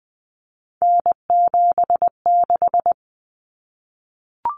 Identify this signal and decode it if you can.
Morse code, heard as a keyed tone, about 20 wpm, 700 Hz. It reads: N76
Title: Common Cut Numbers - 3-Digit 20wpm